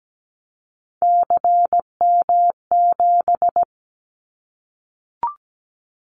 Morse code, heard as a keyed tone, about 17 wpm, 700 Hz. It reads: CM7
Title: Callsign Prefixes for Top 100 DXCC - 17wpm